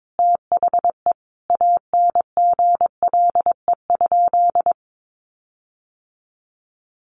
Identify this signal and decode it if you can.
Morse code, heard as a keyed tone, about 22 wpm, 700 Hz. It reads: THE ANGLE?
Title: Sentences from Top 1500 Words 22wpm